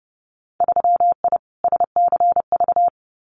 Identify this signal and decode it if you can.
Morse code, heard as a keyed tone, about 30 wpm, 700 Hz. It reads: 3I SCV
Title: Winter Field Day Exchanges Encore 30wpm